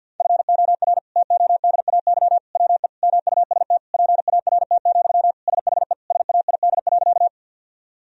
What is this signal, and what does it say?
Morse code, heard as a keyed tone, about 50 wpm, 700 Hz. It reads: FOR TODAY WE MUST PART, SHE SAID.
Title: Sentences from Top 1000 Words - Review Part 1 - 50wpm